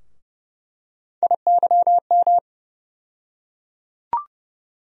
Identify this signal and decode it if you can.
Morse code, heard as a keyed tone, about 30 wpm, 700 Hz. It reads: IYM
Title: USA Call Sign Postfixes - 3 Letters 30wpm